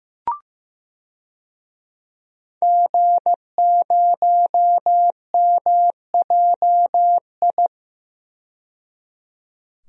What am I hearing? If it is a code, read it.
Morse code, heard as a keyed tone, about 15 wpm, 700 Hz. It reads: G0MJI
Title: DXCC-100 Call Signs - 5 Characters - 15wpm